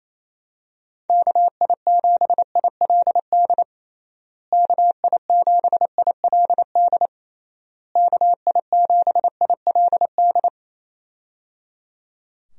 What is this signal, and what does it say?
Morse code, heard as a keyed tone, about 28 wpm, 700 Hz. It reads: KI7ILD KI7ILD KI7ILD
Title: USA Call Signs - 6 Characters - Repeated 3x - 28wpm